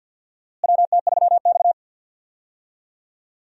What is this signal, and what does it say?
Morse code, heard as a keyed tone, about 47 wpm, 700 Hz. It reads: WT3X
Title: USA Call Signs - 4 Characters - Repeated 2x - 50wpm